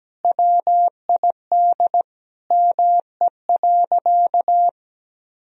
Morse code, heard as a keyed tone, about 17 wpm, 700 Hz. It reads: WID ME.
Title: Sentences from Top 500 Words and Abbreviations 17wpm